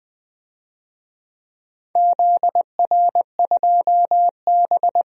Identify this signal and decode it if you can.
Morse code, heard as a keyed tone, about 20 wpm, 700 Hz. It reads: ZR2B
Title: DXCC-100 Call Signs - 4 Characters - 20wpm